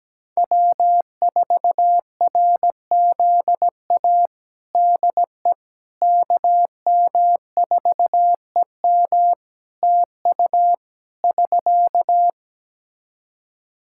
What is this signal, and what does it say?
Morse code, heard as a keyed tone, about 17 wpm, 700 Hz. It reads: W4RZA DE KM4EM TU <SK>
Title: Practice QSO - Exchange 4 - Ending - 17wpm